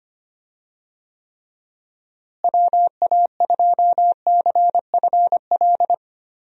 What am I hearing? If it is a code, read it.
Morse code, heard as a keyed tone, about 25 wpm, 700 Hz. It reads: WA2CFL
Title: USA Call Signs - 6 Characters - Repeated 3x - 25wpm